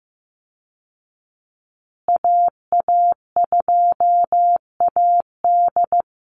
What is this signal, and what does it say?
Morse code, heard as a keyed tone, about 15 wpm, 700 Hz. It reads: AA2AD
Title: Call Signs Repeated 2x - 15wpm